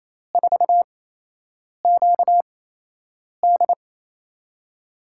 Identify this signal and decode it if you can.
Morse code, heard as a keyed tone, about 28 wpm, spaced as for 8 wpm, 700 Hz. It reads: 4QD
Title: USA Call Signs - 5 Characters - 8x Character Spacing - 28wpm